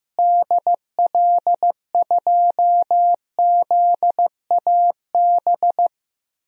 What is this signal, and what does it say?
Morse code, heard as a keyed tone, about 15 wpm, 700 Hz. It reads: DL2ZAB
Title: DXCC-100 Call Signs - 6 Characters - Repeated 3x - 15wpm